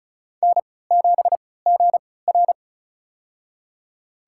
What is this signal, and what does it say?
Morse code, heard as a keyed tone, about 34 wpm, spaced as for 20 wpm, 700 Hz. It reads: N7GR
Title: USA Call Signs - 4 Characters - 3x Character Spacing - 35wpm